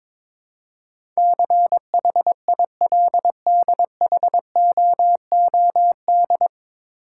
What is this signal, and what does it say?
Morse code, heard as a keyed tone, about 22 wpm, 700 Hz. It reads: CHILDHOOD
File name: 9 Letter Words - First 3 Letters Given - 22wpm